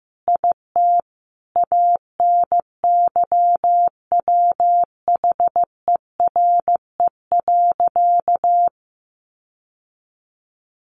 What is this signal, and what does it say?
Morse code, heard as a keyed tone, about 15 wpm, 700 Hz. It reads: IT ANYWHERE.
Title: Sentences from Top 1400 Words 15wpm